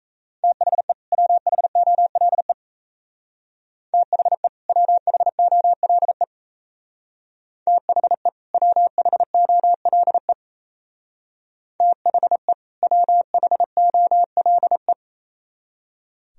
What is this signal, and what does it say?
Morse code, heard as a keyed tone, about 33 wpm, 700 Hz. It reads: THE WHOLE THE WHOLE THE WHOLE THE WHOLE
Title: Speed-Racing - Sets of 2 Words 28wpm